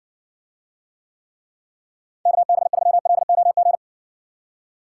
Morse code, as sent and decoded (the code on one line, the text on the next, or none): -.- -... ...-- .-.. --- --.
KB3LOG